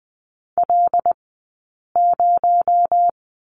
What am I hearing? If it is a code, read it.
Morse code, heard as a keyed tone, about 20 wpm, 700 Hz. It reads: L 0
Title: 217 Lesson 35d - Number 0 - Two Characters 20wpm